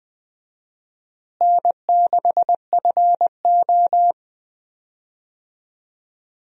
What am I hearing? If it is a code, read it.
Morse code, heard as a keyed tone, about 20 wpm, 700 Hz. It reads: N6FO
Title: USA Call Signs - 4 Characters - Repeated 2x - 20wpm